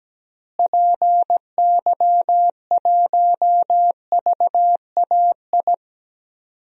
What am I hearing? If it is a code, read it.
Morse code, heard as a keyed tone, about 17 wpm, 700 Hz. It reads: PY1VAI
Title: DXCC-100 Call Signs - 6 Characters - Repeated 2x - 17wpm